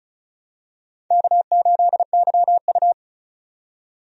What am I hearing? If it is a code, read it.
Morse code, heard as a keyed tone, about 34 wpm, 700 Hz. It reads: K8YU
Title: USA Call Signs - 4 Characters - 35wpm